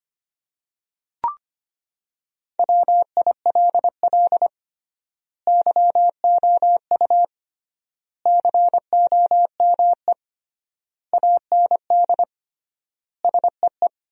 Morse code, heard as a keyed tone, about 25 wpm, 700 Hz. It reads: WILL YOU COME AND SEE
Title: 3x Word-Spacing - Sentences from Top 100 Words - 25wpm